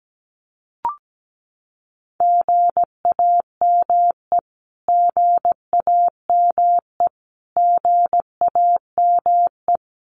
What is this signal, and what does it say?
Morse code, heard as a keyed tone, about 17 wpm, 700 Hz. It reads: GAME GAME GAME
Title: Top 500 Words Repeated 17wpm